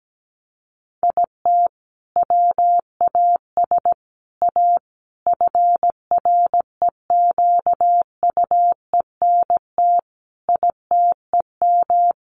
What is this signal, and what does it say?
Morse code, heard as a keyed tone, about 17 wpm, 700 Hz. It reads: IT WAS A FREQUENT ITEM
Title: Sentences from Top 1400 Words 17wpm